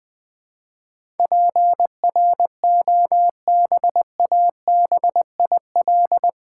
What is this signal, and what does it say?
Morse code, heard as a keyed tone, about 20 wpm, 700 Hz. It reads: PROBABIL
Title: Build A Word - 4 Letters At A Time - 11-Letter Words - 20wpm